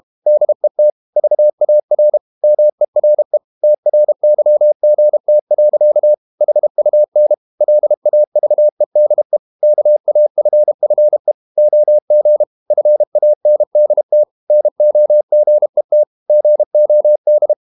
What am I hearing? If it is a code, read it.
Morse code, heard as a keyed tone, about 32 wpm, 600 Hz. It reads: DET VAR MERE TRYGT. HUN LAVEDE KAFFE OG FANDT NOGET GOD
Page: Ferie 32 wpm